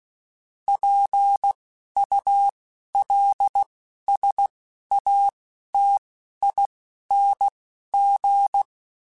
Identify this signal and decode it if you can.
Morse code, heard as a keyed tone, about 16 wpm, spaced as for 11 wpm, 800 Hz. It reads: PULSATING
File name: Technical Subjects / Technical Subjects-mp3s / Sw Pwr Sup @10